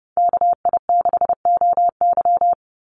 Morse code, heard as a keyed tone, about 30 wpm, 700 Hz. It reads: KI6OY
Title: Call Signs - 30wpm